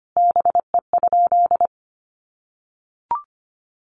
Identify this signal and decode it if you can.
Morse code, heard as a keyed tone, about 25 wpm, 700 Hz. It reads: BE?
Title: Sentences from Top 200 Words 25wpm